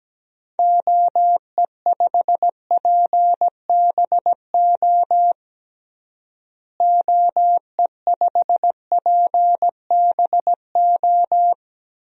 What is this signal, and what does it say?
Morse code, heard as a keyed tone, about 17 wpm, 700 Hz. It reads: OE5PBO OE5PBO
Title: DXCC-100 Call Signs - 6 Characters - Repeated 2x - 17wpm